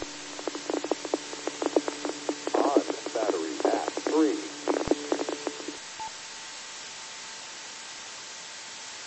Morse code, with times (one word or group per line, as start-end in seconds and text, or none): none